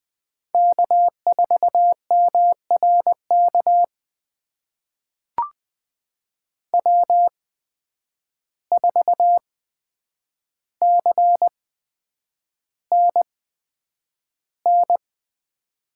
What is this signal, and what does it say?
Morse code, heard as a keyed tone, about 20 wpm, spaced as for 9 wpm, 700 Hz. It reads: K4MRK W4CNN
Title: USA Call Signs - 5 Characters - 8x Character Spacing - 20wpm